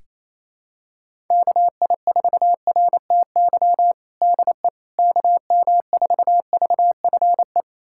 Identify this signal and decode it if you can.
Morse code, heard as a keyed tone, about 27 wpm, 700 Hz. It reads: KI4RTY DE KM4VFE
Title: USA Callsign DE Callsign - 6 Characters - 28wpm